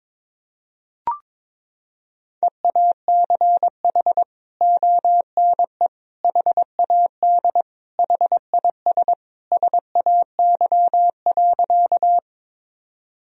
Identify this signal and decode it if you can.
Morse code, heard as a keyed tone, about 22 wpm, 700 Hz. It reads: EACH ONE HAD HIS SAY.
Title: Sentences from Top 1000 Words - Review Part 1 - 22wpm